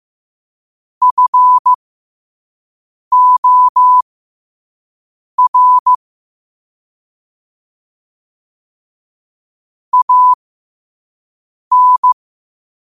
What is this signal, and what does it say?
Morse code, heard as a keyed tone, about 15 wpm, spaced as for 5 wpm, 1000 Hz. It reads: FOR AN
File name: qotdfiles / 5wpm